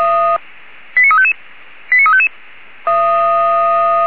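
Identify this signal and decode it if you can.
Five-tone selective call (ZVEI), heard as quick runs of five tones, tones of about 80 ms. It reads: 89355 89355
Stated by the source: ZVEI five-tone selective call received by SDR.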